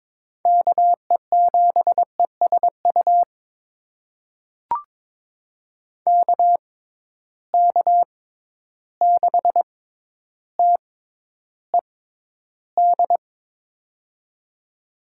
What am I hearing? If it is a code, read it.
Morse code, heard as a keyed tone, about 21 wpm, spaced as for 11 wpm, 700 Hz. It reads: KE7ESU KK6TED
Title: USA Call Signs - 6 Characters - 6x Character Spacing - 22wpm